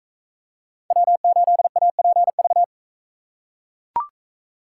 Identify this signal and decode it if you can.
Morse code, heard as a keyed tone, about 40 wpm, 700 Hz. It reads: W8APV